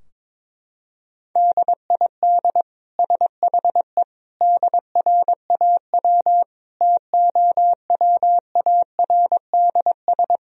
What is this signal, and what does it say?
Morse code, heard as a keyed tone, about 21 wpm, 700 Hz. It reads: DID SHE DRAW TOWARDS
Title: Sentences from Top 1000 Words - Review Part 2 - 22wpm